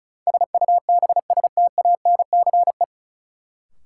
Morse code, heard as a keyed tone, about 36 wpm, 700 Hz. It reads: SUBSTANCE
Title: Sentences from Top 1500 Words - New Word Drill - 35wpm